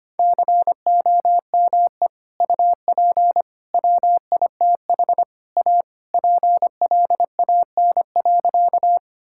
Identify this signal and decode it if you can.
Morse code, heard as a keyed tone, about 25 wpm, 700 Hz. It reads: COME UP WITH A PLAN.
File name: practice / 25wpm / Sentences with 4-Word Phrases - 25wpm